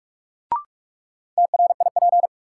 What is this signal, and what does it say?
Morse code, heard as a keyed tone, about 46 wpm, 700 Hz. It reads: TRIP
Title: Sentences from Top 1300 Words - New Word Drill Alt - 45wpm